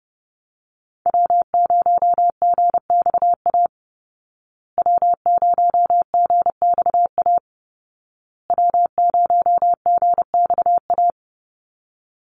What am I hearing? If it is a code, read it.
Morse code, heard as a keyed tone, about 30 wpm, 700 Hz. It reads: W0GXA W0GXA W0GXA
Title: Call Signs Repeated 2x - 30wpm